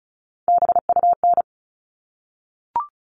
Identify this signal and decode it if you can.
Morse code, heard as a keyed tone, about 34 wpm, 700 Hz. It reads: BUN
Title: All Cut Numbers - 3-Digit 35wpm